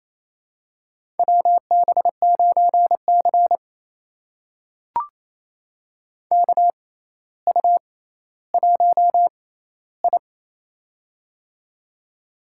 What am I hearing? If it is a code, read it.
Morse code, heard as a keyed tone, about 28 wpm, spaced as for 14 wpm, 700 Hz. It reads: WB9C KU1I
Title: USA Call Signs - 6x Character Spacing - 28wpm